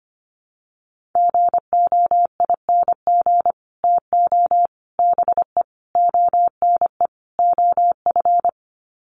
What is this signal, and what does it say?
Morse code, heard as a keyed tone, about 25 wpm, 700 Hz. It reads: GOING TO BE ONE OF
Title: Sets of 5 Words Encore2 25wpm